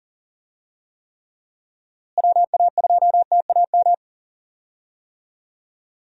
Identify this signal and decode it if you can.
Morse code, heard as a keyed tone, about 40 wpm, 700 Hz. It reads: WA2TAM
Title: USA Call Signs - 6 Characters - Repeated 3x - 40wpm